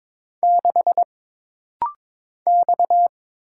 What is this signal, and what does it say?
Morse code, heard as a keyed tone, about 22 wpm, 700 Hz. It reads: 6 X
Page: This rapid-fire timing is geared to push your ICR (Instant Character Recognition) skills.